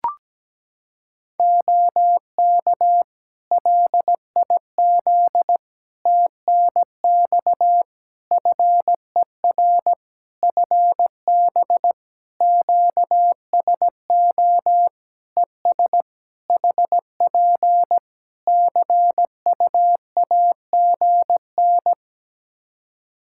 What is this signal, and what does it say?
Morse code, heard as a keyed tone, about 17 wpm, 700 Hz. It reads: OK LIZ TNX FER FB QSO ES HP CUAGN
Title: K1USN SST QSO Practice 17wpm